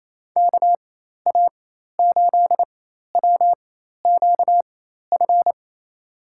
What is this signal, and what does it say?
Morse code, heard as a keyed tone, about 28 wpm, spaced as for 13 wpm, 700 Hz. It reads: KA8WQF
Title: USA Call Signs - 6 Characters - 4x Character Spacing - 28wpm